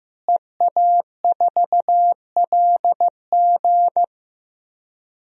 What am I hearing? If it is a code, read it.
Morse code, heard as a keyed tone, about 15 wpm, 700 Hz. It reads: EA4LG
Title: DXCC-100 Call Signs - 5 Characters - Repeated 3x - 15wpm